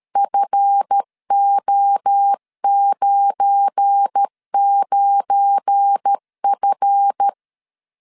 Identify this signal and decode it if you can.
Morse code, heard as a keyed tone, about 13 wpm, 800 Hz. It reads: FO99F
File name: LEZIONI / 7 GF9Z / 7 GF9 12wpm 2319